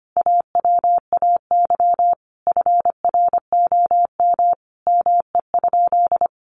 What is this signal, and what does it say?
Morse code, heard as a keyed tone, about 25 wpm, 700 Hz. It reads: AWAY FROM ME?
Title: Sentences from Top 1500 Words 25wpm